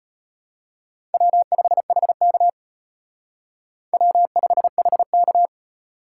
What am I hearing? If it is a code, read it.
Morse code, heard as a keyed tone, about 32 wpm, 700 Hz. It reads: W5HK W5HK